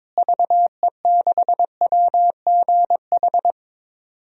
Morse code, heard as a keyed tone, about 22 wpm, 700 Hz. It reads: VE6WGH
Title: DXCC-100 Call Signs - 6 Characters - Repeated 2x - 22wpm